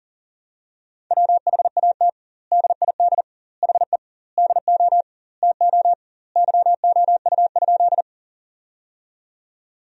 Morse code, heard as a keyed tone, about 40 wpm, 700 Hz. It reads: WHAT DID HE DO TO YOU?
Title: 2x Word-Spacing - Sentences from Top 100 Words - 40wpm